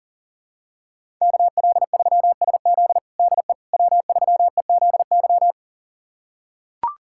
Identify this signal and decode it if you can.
Morse code, heard as a keyed tone, about 40 wpm, 700 Hz. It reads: KL3SZ DE W3EZY
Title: USA Callsign DE Callsign - 5 Characters - 40wpm